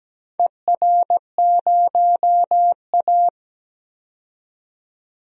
Morse code, heard as a keyed tone, about 17 wpm, 700 Hz. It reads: ER0A
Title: DXCC-100 Call Signs - 4 Characters - 17wpm